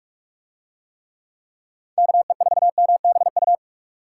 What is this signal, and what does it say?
Morse code, heard as a keyed tone, about 45 wpm, 700 Hz. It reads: KE4MBU